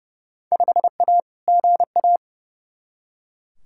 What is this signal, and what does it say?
Morse code, heard as a keyed tone, about 30 wpm, 700 Hz. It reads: 5A GA
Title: Field Day Exchanges Encore - 30wpm